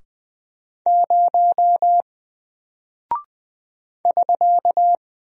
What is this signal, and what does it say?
Morse code, heard as a keyed tone, about 20 wpm, 700 Hz. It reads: 0 <SK>
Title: Single Character (Letter:Number:Punctuation:Prosign) 20wpm